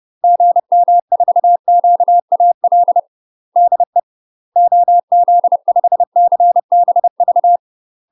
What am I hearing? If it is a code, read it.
Morse code, heard as a keyed tone, about 30 wpm, 700 Hz. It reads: GM4QAL DE OZ5CBV
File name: UK MASTER QSOs TRAINING / 30WPM / UK MASTER QSOs 0013